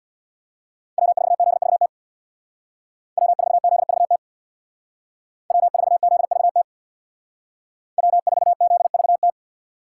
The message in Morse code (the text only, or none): W4ZVT W4ZVT W4ZVT W4ZVT